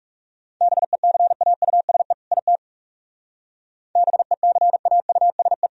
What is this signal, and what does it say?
Morse code, heard as a keyed tone, about 37 wpm, 700 Hz. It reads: BECAUSE IT BECAUSE
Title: Speed-Racing - Sets of 2 Words 30wpm